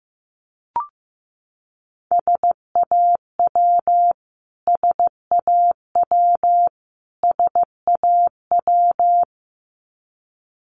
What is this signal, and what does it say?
Morse code, heard as a keyed tone, about 15 wpm, 700 Hz. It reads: SAW SAW SAW
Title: Top 500 Words Repeated 15wpm